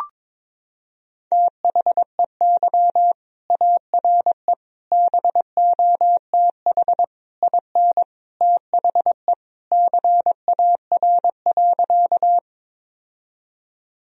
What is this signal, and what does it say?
Morse code, heard as a keyed tone, about 22 wpm, 700 Hz. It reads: THEY ARE BOTH IN THE CAR.
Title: Sentences from Top 1000 Words - Review Part 1 - 22wpm